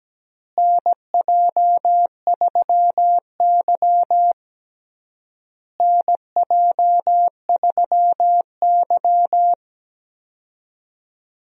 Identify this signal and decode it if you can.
Morse code, heard as a keyed tone, about 17 wpm, 700 Hz. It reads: NJ3Y NJ3Y
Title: USA Call Signs - 4 Characters - Repeated 3x - 17wpm